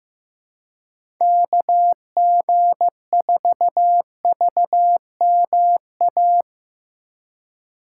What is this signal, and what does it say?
Morse code, heard as a keyed tone, about 16 wpm, 700 Hz. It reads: KG4VMA
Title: USA Call Signs - 6 Characters - 2x Character Spacing - 15wpm